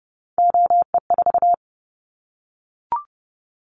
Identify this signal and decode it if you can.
Morse code, heard as a keyed tone, about 30 wpm, 700 Hz. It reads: OE4
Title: Callsign Prefixes for Top 100 DXCC - 30wpm